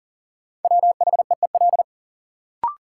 Morse code, heard as a keyed tone, about 39 wpm, 700 Hz. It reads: WHEEL
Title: Sentences from Top 500 Words - New Word Drill - 40wpm